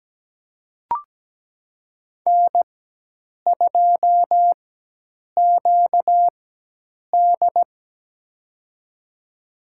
Morse code, heard as a keyed tone, about 17 wpm, spaced as for 8 wpm, 700 Hz. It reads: N2QD
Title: USA Call Signs - 4 Characters - 4x Character Spacing - 17wpm